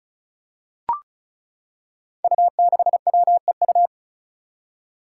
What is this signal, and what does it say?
Morse code, heard as a keyed tone, about 36 wpm, 700 Hz. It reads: U6WEU
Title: Five Challenging Letter-Number Combinations 35wpm